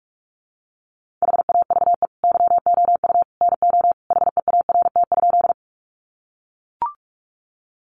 Morse code, heard as a keyed tone, about 44 wpm, 700 Hz. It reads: HAVE YOU NO HEART?
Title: Sentences from Top 1200 Words 45wpm